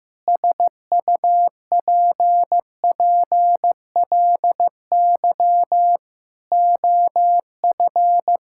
Morse code, heard as a keyed tone, about 15 wpm, 700 Hz. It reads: SUPPLY OF
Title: Sentences from Top 1000 Words - Review Part 5 - 15wpm